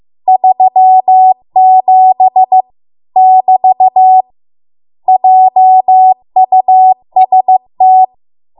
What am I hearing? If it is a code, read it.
Morse code, heard as a keyed tone, about 15 wpm, 750 Hz. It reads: 37 <BT> JUST